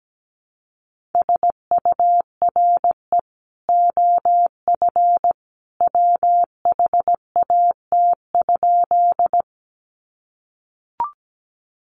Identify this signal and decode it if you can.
Morse code, heard as a keyed tone, about 17 wpm, 700 Hz. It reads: SURE OF WHAT?
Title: Sentences from Top 300 Words 17wpm